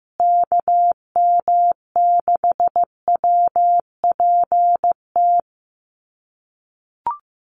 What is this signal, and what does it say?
Morse code, heard as a keyed tone, about 15 wpm, 700 Hz. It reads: KM6WPT
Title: USA Call Signs Repeated 3x 15wpm